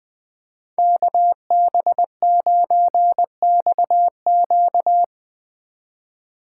Morse code, heard as a keyed tone, about 20 wpm, 700 Hz. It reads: KB9XQ
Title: Speed Racing - USA 2x2 Call Signs 17wpm